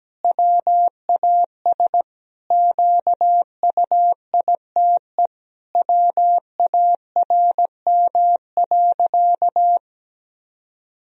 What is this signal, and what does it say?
Morse code, heard as a keyed tone, about 17 wpm, 700 Hz. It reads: WAS QUITE WARM.
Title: Sentences from Top 1000 Words - Review Part 2 - 17wpm